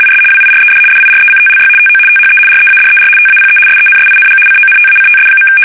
RTTY, about 75 baud. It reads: IS WIRED DIRECTLY TO THE READER TAPE SENSING CONTACTS.
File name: fsk850-75B / teletype-corp